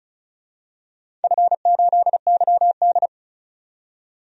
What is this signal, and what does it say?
Morse code, heard as a keyed tone, about 34 wpm, 700 Hz. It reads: F8YD
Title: DXCC-100 Call Signs - 4 Characters - 35wpm